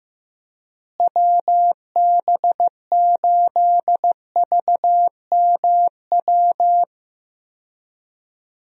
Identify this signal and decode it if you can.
Morse code, heard as a keyed tone, about 15 wpm, 700 Hz. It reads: WB8VMW
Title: USA Call Signs - 6 Characters - Repeated 2x - 15wpm